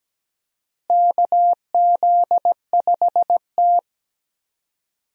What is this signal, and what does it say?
Morse code, heard as a keyed tone, about 17 wpm, 700 Hz. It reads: KZ5T
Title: USA Call Signs - 4 Characters - Repeated 2x - 17wpm